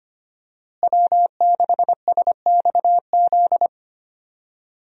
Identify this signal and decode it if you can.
Morse code, heard as a keyed tone, about 25 wpm, 700 Hz. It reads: W6SXZ
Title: USA Call Signs - 5 Characters - Repeated 3x - 25wpm